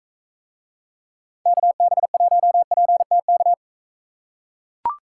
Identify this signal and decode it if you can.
Morse code, heard as a keyed tone, about 41 wpm, 700 Hz. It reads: KB1PTK